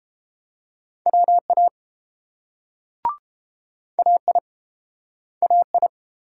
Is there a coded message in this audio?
Yes: Morse code from 1.1 to 5.9 s, heard as a keyed tone, about 30 wpm, 700 Hz. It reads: WA AI AI